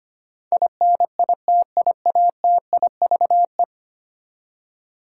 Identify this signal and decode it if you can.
Morse code, heard as a keyed tone, about 25 wpm, 700 Hz. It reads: INITIATIVE
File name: Ten Letter Words - 25wpm